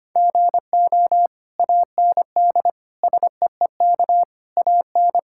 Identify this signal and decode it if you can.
Morse code, heard as a keyed tone, about 25 wpm, 700 Hz. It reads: GO AND SEEK AN
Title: Sentences from Top 1600 Words 25wpm